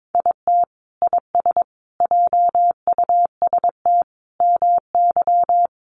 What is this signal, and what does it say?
Morse code, heard as a keyed tone, about 22 wpm, 700 Hz. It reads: IT IS JUST MY
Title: Sentences from Top 600 Words 22wpm